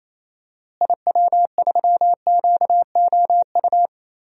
Morse code, heard as a keyed tone, about 27 wpm, 700 Hz. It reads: IW3QOU
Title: DXCC-100 Call Signs - 6 Characters - Repeated 3x - 28wpm